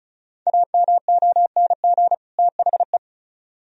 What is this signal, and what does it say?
Morse code, heard as a keyed tone, about 44 wpm, 700 Hz. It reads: AMONG THE
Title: Speed-Racing - Sets of 2 Words 35wpm